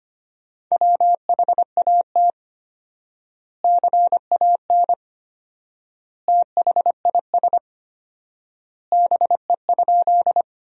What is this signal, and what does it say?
Morse code, heard as a keyed tone, about 25 wpm, 700 Hz. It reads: WHAT CAN THIS BE?
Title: 4x Word-Spacing - Sentences from Top 100 Words - 25wpm